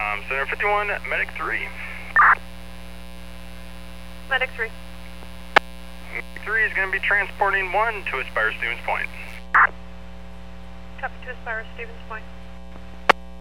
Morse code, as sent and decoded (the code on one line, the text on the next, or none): none